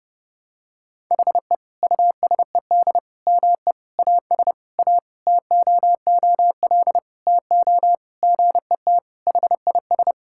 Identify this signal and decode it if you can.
Morse code, heard as a keyed tone, about 32 wpm, 700 Hz. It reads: HE USED ME AS A TOOL TO GET HIS
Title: Sentences from Top 1000 Words - Review Part 6 - 30wpm